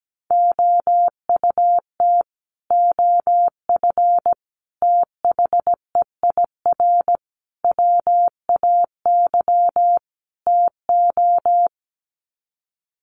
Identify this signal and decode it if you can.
Morse code, heard as a keyed tone, about 17 wpm, 700 Hz. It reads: OUT OF THEIR WAY TO
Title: Sets of 5 Words Encore2 17wpm